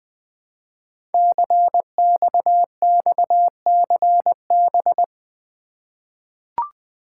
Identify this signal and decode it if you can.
Morse code, heard as a keyed tone, about 21 wpm, 700 Hz. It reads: CXXCB
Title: Code Groups - 5 Letters - 20wpm